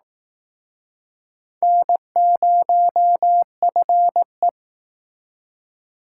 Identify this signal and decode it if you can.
Morse code, heard as a keyed tone, about 19 wpm, 700 Hz. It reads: N0FE
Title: Speed Racing - USA 1x2 Call Signs 15wpm